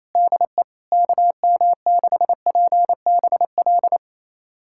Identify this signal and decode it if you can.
Morse code, heard as a keyed tone, about 28 wpm, 700 Hz. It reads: DE KM6PBL
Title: USA Callsign DE Callsign - 6 Characters - 28wpm